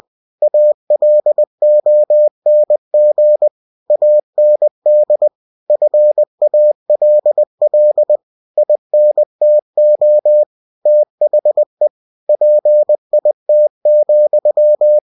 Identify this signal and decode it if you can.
Morse code, heard as a keyed tone, about 20 wpm, 600 Hz. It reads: ALONG AND FALL INTO THE PIT,
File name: winnie-20wpm